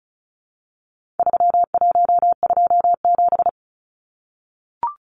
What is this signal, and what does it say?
Morse code, heard as a keyed tone, about 34 wpm, 700 Hz. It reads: 3127
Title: Numbers - 4-Digit 35wpm